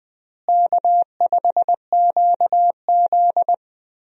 Morse code, heard as a keyed tone, about 21 wpm, 700 Hz. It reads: K5QZ
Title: USA Call Signs - 4 Characters - Repeated 2x - 20wpm